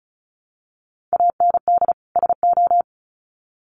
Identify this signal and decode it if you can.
Morse code, heard as a keyed tone, about 34 wpm, 700 Hz. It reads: AND SO
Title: Sets of 2 Words 35wpm